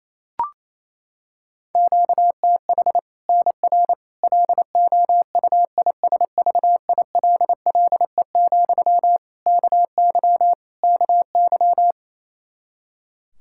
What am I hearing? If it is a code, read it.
Morse code, heard as a keyed tone, about 28 wpm, 700 Hz. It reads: QTH NR LOUISVILLE, KY KY
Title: K1USN SST QSO Practice 28wpm